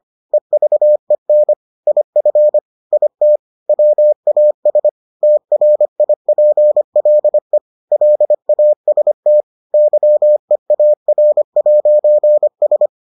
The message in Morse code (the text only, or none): EVEN IF IT WAS TRIPLE LAST YEAR'S